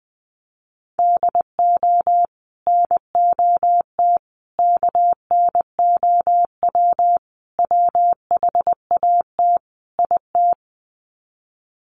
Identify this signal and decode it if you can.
Morse code, heard as a keyed tone, about 20 wpm, 700 Hz. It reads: DO NOT KNOW WHAT IT
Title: Sets of 5 Words 20wpm